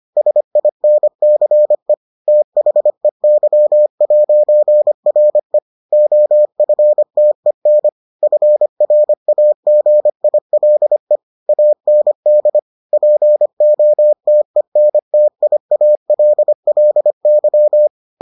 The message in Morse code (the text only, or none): SINCE THEY'RE OFTEN FRAGILE AND POTENTIALLY